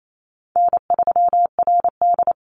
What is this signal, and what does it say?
Morse code, heard as a keyed tone, about 28 wpm, 700 Hz. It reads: N3RD
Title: Call Signs - 28wpm